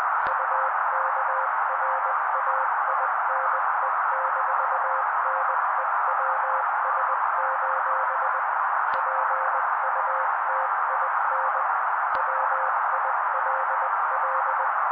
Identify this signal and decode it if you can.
Morse code, heard as a keyed tone, about 20 wpm, 530 Hz. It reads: UKRAINE-NEWS: PUTIN WILL
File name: daily / 18wpm noise